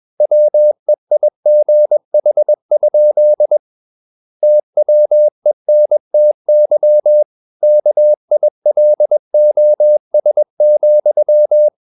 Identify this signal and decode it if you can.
Morse code, heard as a keyed tone, about 21 wpm, 600 Hz. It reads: WEIGH? TWENTY KILOS,
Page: CW med Gnister